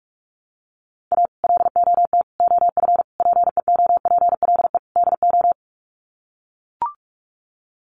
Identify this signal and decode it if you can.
Morse code, heard as a keyed tone, about 44 wpm, 700 Hz. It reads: A LOT OF PEOPLE DO
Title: Sets of 5 Words 45wpm